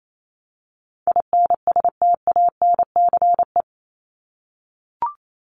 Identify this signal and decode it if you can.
Morse code, heard as a keyed tone, about 28 wpm, 700 Hz. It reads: INSTANCE
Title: Top 501-750 Words 28wpm